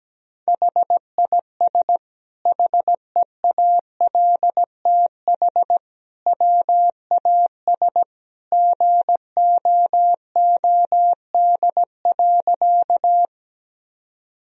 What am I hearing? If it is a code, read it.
Morse code, heard as a keyed tone, about 17 wpm, 700 Hz. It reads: HIS HEALTH WAS GOOD.
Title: Sentences from Top 1000 Words - Review Part 3 - 17wpm